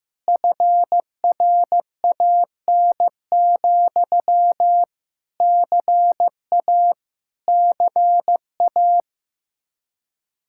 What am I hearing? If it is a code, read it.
Morse code, heard as a keyed tone, about 15 wpm, 700 Hz. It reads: FRAN, CA CA
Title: Speed-Racing Common QSO Phrases Encore - 15wpm